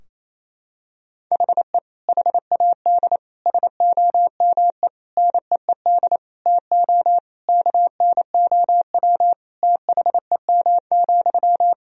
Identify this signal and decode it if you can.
Morse code, heard as a keyed tone, about 27 wpm, 700 Hz. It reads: HE HAD SOME NEED TO KNOW THEM,
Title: Sentences from Top 1000 Words - Review Part 3 - 28wpm